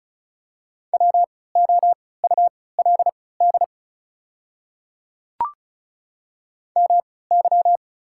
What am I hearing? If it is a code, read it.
Morse code, heard as a keyed tone, about 36 wpm, spaced as for 20 wpm, 690 Hz. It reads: WOULD MY
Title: 3x Character Spacing - Top 100 Words - 35wpm